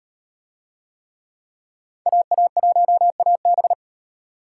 Morse code, heard as a keyed tone, about 38 wpm, 700 Hz. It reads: AA1AB